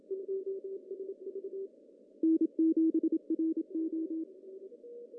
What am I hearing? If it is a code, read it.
Morse code, heard as a keyed tone, about 27 wpm, 320 Hz. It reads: N7RO
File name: CW / cqww-cw-k5tr